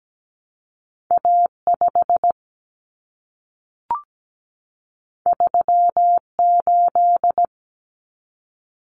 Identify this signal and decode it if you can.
Morse code, heard as a keyed tone, about 17 wpm, 700 Hz. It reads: A 5 3 8
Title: Common Cut Numbers - 2-Digit 17wpm